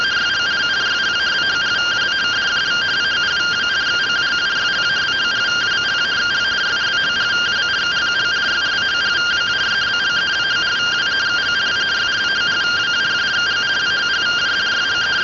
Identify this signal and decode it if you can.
Two-tone teleprinter signal, about 170 Hz shift.